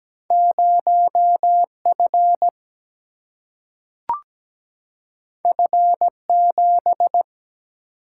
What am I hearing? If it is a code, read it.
Morse code, heard as a keyed tone, about 17 wpm, 700 Hz. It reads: 0F F7
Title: Single-Letter Single-Number 17wpm